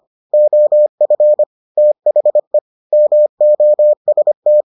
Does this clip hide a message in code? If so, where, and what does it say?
Yes: Morse code from 0.3 to 4.6 s, heard as a keyed tone, about 25 wpm, 600 Hz. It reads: OF THE MOST